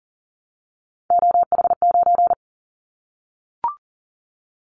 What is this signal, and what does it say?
Morse code, heard as a keyed tone, about 40 wpm, 700 Hz. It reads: OH9
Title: Callsign Prefixes for Top 25 DXCC - 40wpm